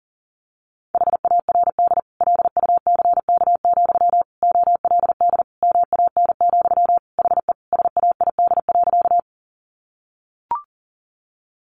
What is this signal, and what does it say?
Morse code, heard as a keyed tone, about 40 wpm, 700 Hz. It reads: HARD LUCK, OLD MAN, HE SAID.
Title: Sentences from Top 1500 Words 40wpm